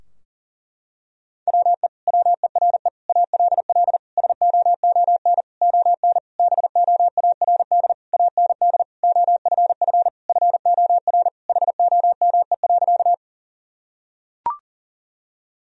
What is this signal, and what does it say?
Morse code, heard as a keyed tone, about 40 wpm, 700 Hz. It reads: WE WERE ALL SOON ON BOARD AND OFF FOR HOME.
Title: Sentences from Top 1000 Words - Review Part 4 - 40wpm